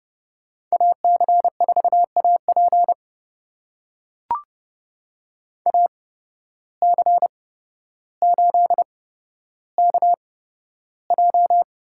Morse code, heard as a keyed tone, about 30 wpm, spaced as for 13 wpm, 700 Hz. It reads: AC4AP AC8KJ
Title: USA Call Signs - 5 Characters - 8x Character Spacing - 30wpm